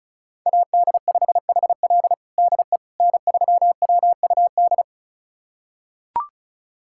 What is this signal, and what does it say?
Morse code, heard as a keyed tone, about 36 wpm, 700 Hz. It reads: AD5HL DE N3WUD
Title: USA Callsign DE Callsign - 5 Characters - 35wpm